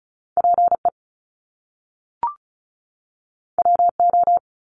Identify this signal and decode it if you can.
Morse code, heard as a keyed tone, about 35 wpm, 700 Hz. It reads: P E W O
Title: Top Two Letter Combinations in Words 35wpm